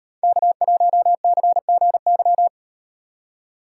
Morse code, heard as a keyed tone, about 38 wpm, 700 Hz. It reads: K1CGY